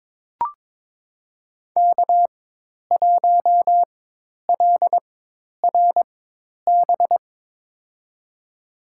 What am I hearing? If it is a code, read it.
Morse code, heard as a keyed tone, about 23 wpm, spaced as for 10 wpm, 700 Hz. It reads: K1LRB
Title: USA Call Signs - 5 Characters - 4x Character Spacing - 22wpm